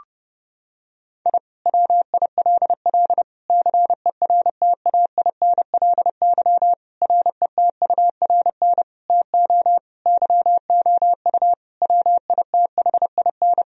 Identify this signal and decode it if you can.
Morse code, heard as a keyed tone, about 32 wpm, 700 Hz. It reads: I WILL CERTAINLY RETURN TO YOU WITHIN
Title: Sentences from Top 1000 Words - Review Part 4 - 30wpm